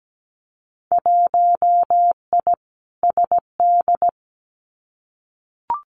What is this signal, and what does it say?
Morse code, heard as a keyed tone, about 17 wpm, 700 Hz. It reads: 1I SD
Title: Winter Field Day Exchanges 17wpm